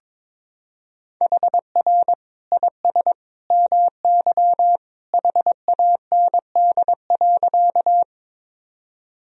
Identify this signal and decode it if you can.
Morse code, heard as a keyed tone, about 22 wpm, 700 Hz. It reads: HR IS MY HAND.
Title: Sentences from Top 200 Words and Abbreviations 22wpm